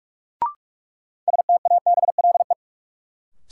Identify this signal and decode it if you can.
Morse code, heard as a keyed tone, about 42 wpm, 700 Hz. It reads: STABLE
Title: Sentences from Top 1500 Words - New Word Drill - 45wpm